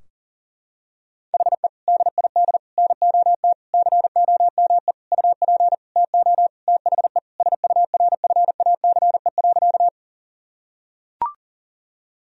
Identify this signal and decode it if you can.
Morse code, heard as a keyed tone, about 40 wpm, 700 Hz. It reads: HE DID NOT COME UP TO THE SURFACE.
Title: Sentences from Top 1000 Words - Review Part 3 - 40wpm